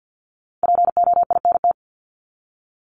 Morse code, heard as a keyed tone, about 50 wpm, 700 Hz. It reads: POINT
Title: ES:S Suffix - Top 100 Words with Base Words 50wpm